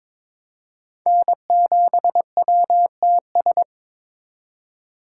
Morse code, heard as a keyed tone, about 22 wpm, 700 Hz. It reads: N7WTS
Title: USA Call Signs - 4x Character Spacing - 22wpm